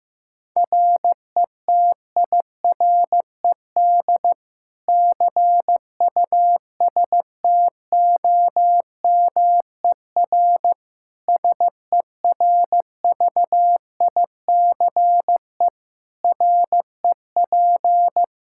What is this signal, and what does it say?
Morse code, heard as a keyed tone, about 16 wpm, 700 Hz. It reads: RETIRED CUSTOMER SERVICE REP
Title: K1USN SST QSO Practice 15wpm